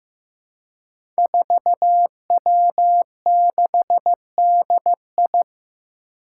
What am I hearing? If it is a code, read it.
Morse code, heard as a keyed tone, about 15 wpm, 700 Hz. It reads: 4W6DI
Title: Five Challenging Letter-Number Combinations 15wpm